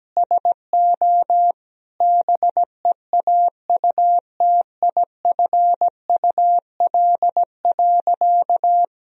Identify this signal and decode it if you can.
Morse code, heard as a keyed tone, about 17 wpm, 700 Hz. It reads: SO BEAUTIFUL.
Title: Sentences from Top 1000 Words - Review Part 3 - 17wpm